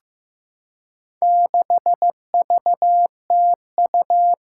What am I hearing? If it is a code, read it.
Morse code, heard as a keyed tone, about 15 wpm, 700 Hz. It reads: 6VTU
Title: Four Challenging Letter-Number Combinations 15wpm